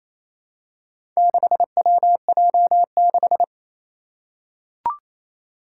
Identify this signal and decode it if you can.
Morse code, heard as a keyed tone, about 27 wpm, 700 Hz. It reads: 6WJ6
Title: Four Challenging Letter-Number Combinations 28wpm